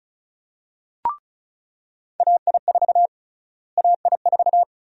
Morse code, heard as a keyed tone, about 36 wpm, 700 Hz. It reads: AI4 AI4
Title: USA Call Sign Prefixes - Repeated 2x - 35wpm